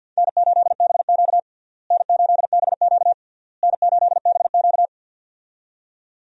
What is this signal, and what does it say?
Morse code, heard as a keyed tone, about 51 wpm, 700 Hz. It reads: N8BQ N8BQ N8BQ
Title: USA Call Signs - 4 Characters - Repeated 3x - 50wpm